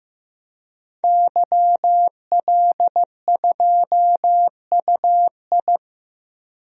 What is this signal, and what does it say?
Morse code, heard as a keyed tone, about 15 wpm, 700 Hz. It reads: YL2UI
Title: DXCC-100 Call Signs - 5 Characters - 15wpm